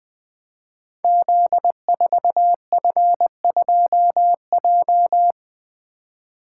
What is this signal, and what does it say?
Morse code, heard as a keyed tone, about 20 wpm, 700 Hz. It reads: Z4F2J
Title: Code Groups - 5 Letter:Numbers - 20wpm